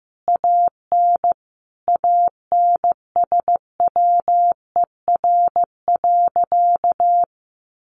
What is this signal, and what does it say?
Morse code, heard as a keyed tone, about 15 wpm, 700 Hz. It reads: AN ANSWER.
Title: Sentences from Top 1200 Words 15wpm